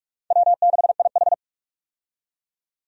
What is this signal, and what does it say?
Morse code, heard as a keyed tone, about 45 wpm, 700 Hz. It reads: W6IH
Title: USA Call Signs - 4 Characters - 45wpm